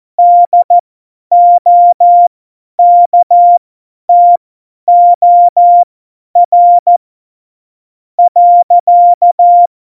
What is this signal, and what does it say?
Morse code, heard as a keyed tone, about 14 wpm, spaced as for 10 wpm, 700 Hz. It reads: DOKTOR .
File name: Morsekurs / Witze / Witz 27 March 2018, 50 F70